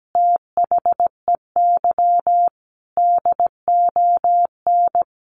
Morse code, heard as a keyed tone, about 17 wpm, 700 Hz. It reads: THEY DON
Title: Easy Phrases 17wpm